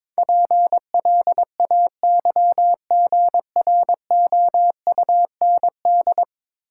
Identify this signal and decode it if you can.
Morse code, heard as a keyed tone, about 22 wpm, 700 Hz. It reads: PLAYGROUND
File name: Easy Words - Speed-Racing with Hints - 22wpm